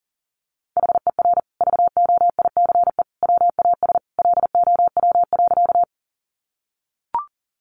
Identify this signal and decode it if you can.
Morse code, heard as a keyed tone, about 40 wpm, 700 Hz. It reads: HER VOICE WAS LOW.
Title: Sentences from Top 500 Words 40wpm